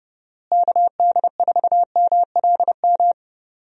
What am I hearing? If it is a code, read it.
Morse code, heard as a keyed tone, about 30 wpm, 700 Hz. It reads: KD4MLM
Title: USA Call Signs - 6 Characters - Repeated 2x - 30wpm